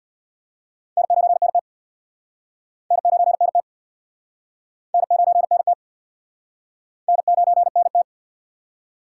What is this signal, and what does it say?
Morse code, heard as a keyed tone, about 66 wpm, 700 Hz. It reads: N9NT N9NT N9NT N9NT